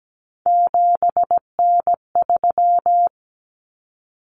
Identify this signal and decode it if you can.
Morse code, heard as a keyed tone, about 17 wpm, 690 Hz. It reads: 7N3
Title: Common Cut Numbers - 3-Digit 17wpm